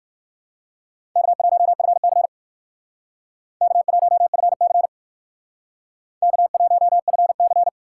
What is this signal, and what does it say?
Morse code, heard as a keyed tone, about 54 wpm, 700 Hz. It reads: K1FC K1FC K1FC
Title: Speed Racing - USA 1x2 Call Signs 45wpm